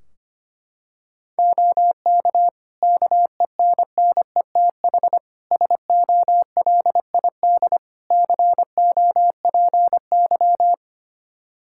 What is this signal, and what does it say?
Morse code, heard as a keyed tone, about 25 wpm, 700 Hz. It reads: OK KENNETH SOLID COPY
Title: Practice QSO - Exchange 3 - More Info - 25wpm